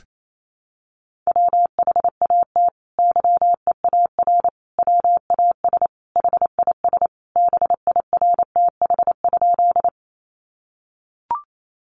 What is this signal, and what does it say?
Morse code, heard as a keyed tone, about 28 wpm, 700 Hz. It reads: WHAT YEAR WAS HIS BIRTH?
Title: Sentences from Top 800 Words 28wpm